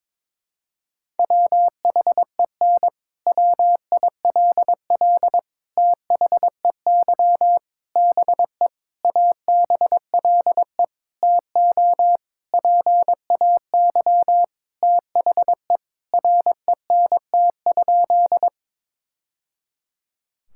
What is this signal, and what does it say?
Morse code, heard as a keyed tone, about 22 wpm, 700 Hz. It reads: WHEN WILL THEY BE ABLE TO PAY THE RENT?
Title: Sentences from Top 1000 Words - Review Part 5 - 22wpm